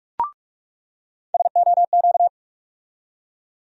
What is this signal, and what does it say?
Morse code, heard as a keyed tone, about 46 wpm, 700 Hz. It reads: SOQ
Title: USA Call Sign Postfixes - 3 Letters 45wpm